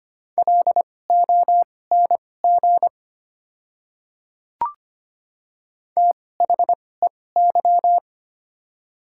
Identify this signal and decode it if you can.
Morse code, heard as a keyed tone, about 25 wpm, spaced as for 18 wpm, 700 Hz. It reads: LONG THEY
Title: 2x Character Spacing - Top 100 Words - 25wpm